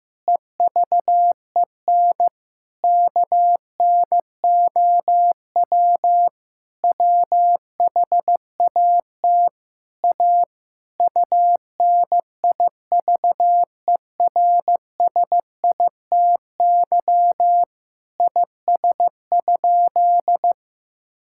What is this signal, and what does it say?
Morse code, heard as a keyed tone, about 16 wpm, 700 Hz. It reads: EVEN KNOW WHAT A UNIVERSITY IS?
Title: Sentences from Top 1000 Words - Review Part 5 - 15wpm